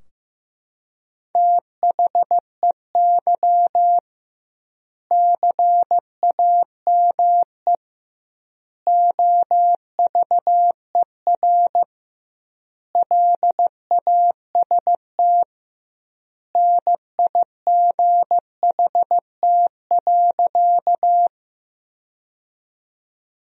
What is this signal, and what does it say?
Morse code, heard as a keyed tone, about 16 wpm, 700 Hz. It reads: THEY CAME OVER LAST NIGHT.
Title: 2x Word-Spacing - Sentences from Top 300 Words - 15wpm